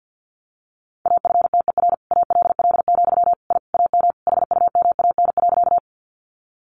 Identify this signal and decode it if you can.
Morse code, heard as a keyed tone, about 50 wpm, 700 Hz. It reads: AFTER ALL, I AM HUMAN.
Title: Sentences from Top 600 Words 50wpm